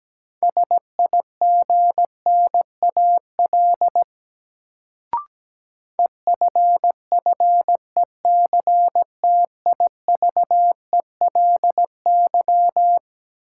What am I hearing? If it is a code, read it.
Morse code, heard as a keyed tone, about 17 wpm, 700 Hz. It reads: SIGNAL EFFECTIVELY
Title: Sentences from Top 1400 Words - New Word Drill Alt - 17wpm